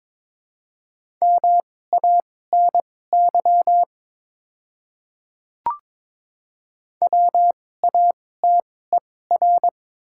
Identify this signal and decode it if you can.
Morse code, heard as a keyed tone, about 22 wpm, spaced as for 16 wpm, 700 Hz. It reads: MANY WATER
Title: 2x Character Spacing - Top 200 Words - 22wpm